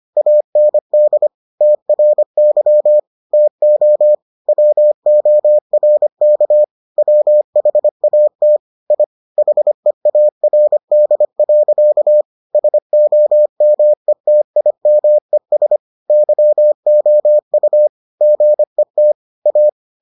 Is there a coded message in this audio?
Yes: Morse code from 0.2 to 19.7 s, heard as a keyed tone, about 25 wpm, 600 Hz. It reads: AND TRY TO WORK WHAT I HEARD. SOMETIMES YOU GET A